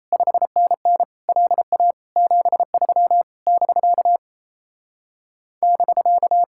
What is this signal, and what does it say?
Morse code, heard as a keyed tone, about 37 wpm, 700 Hz. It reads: 5NN LA 73 <BK> <BK>